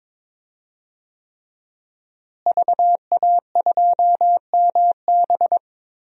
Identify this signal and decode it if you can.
Morse code, heard as a keyed tone, about 22 wpm, 700 Hz. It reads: VA2MB
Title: DXCC-100 Call Signs - 5 Characters - Repeated 3x - 22wpm